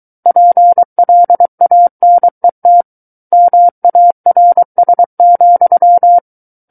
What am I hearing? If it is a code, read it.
Morse code, heard as a keyed tone, about 23 wpm, 700 Hz. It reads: PLANET MARS,
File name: Book One / 14-Chapter 14 - 23 WPM